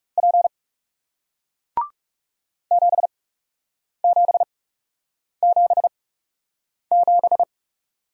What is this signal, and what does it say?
Morse code, heard as a keyed tone, about 38 wpm, spaced as for 8 wpm, 700 Hz. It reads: P 7777